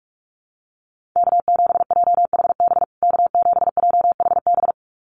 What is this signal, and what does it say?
Morse code, heard as a keyed tone, about 44 wpm, 700 Hz. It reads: K7JHB K7JHB
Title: USA Call Signs Repeated 2x 45wpm